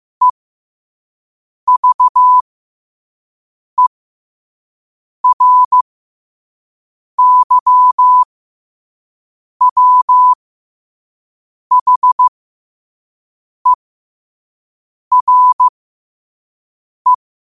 Morse code, heard as a keyed tone, about 15 wpm, spaced as for 5 wpm, 1000 Hz. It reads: EVERYWHERE